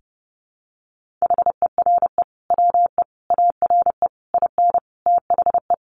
Morse code, heard as a keyed tone, about 28 wpm, 700 Hz. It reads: HERE WE ARE IN THE
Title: Sentences from Top 1400 Words 30wpm